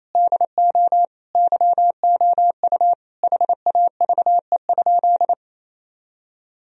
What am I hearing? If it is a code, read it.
Morse code, heard as a keyed tone, about 28 wpm, 700 Hz. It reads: DO YOU HAVE?
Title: Sentences from Top 1000 Words - Review Part 3 - 28wpm